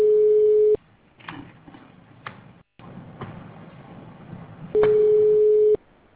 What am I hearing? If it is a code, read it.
Ringback tone (Europe, 425 Hz).